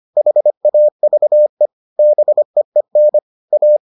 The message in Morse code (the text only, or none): HAVE BEEN A